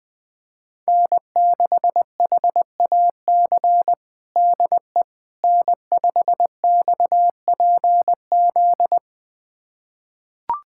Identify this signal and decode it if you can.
Morse code, heard as a keyed tone, about 20 wpm, 700 Hz. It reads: N6HAC DE N5XPZ
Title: USA Callsign DE Callsign - 5 Characters - 20wpm